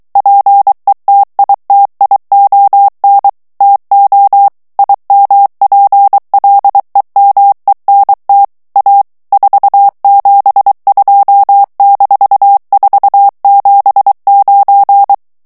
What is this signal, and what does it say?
Morse code, heard as a keyed tone, about 23 wpm, 800 Hz. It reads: PETITION TO IMPLEMENT A 472-479
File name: ANGLAIS texte clair 3 vitesse 22 mots minute